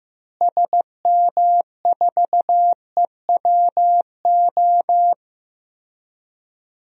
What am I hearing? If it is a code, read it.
Morse code, heard as a keyed tone, about 15 wpm, 700 Hz. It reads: SM4EWO
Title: DXCC-100 Call Signs - 6 Characters - Repeated 3x - 15wpm